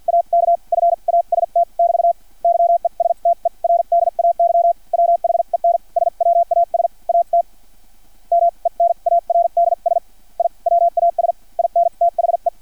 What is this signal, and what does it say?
Morse code, heard as a keyed tone, about 50 wpm, spaced as for 41 wpm, 680 Hz. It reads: AK FAST <BT> YESTERDAY WHEN SWAS AT MENARDS I WAS INTHE
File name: archived / CRSnet